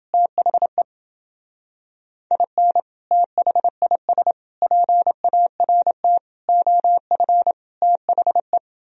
Morse code, heard as a keyed tone, about 29 wpm, 700 Hz. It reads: THE IN THIS PART OF THE